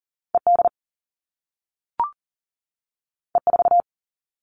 Morse code, heard as a keyed tone, about 42 wpm, 700 Hz. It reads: ED E4
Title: All Cut Numbers - 2-Digit 40wpm